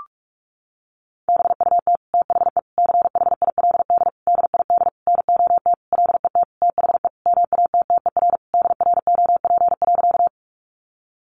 Morse code, heard as a keyed tone, about 44 wpm, 700 Hz. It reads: BUT THE CHILD DID NOT LET THE MATTER DROP.
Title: Sentences from Top 1400 Words 45wpm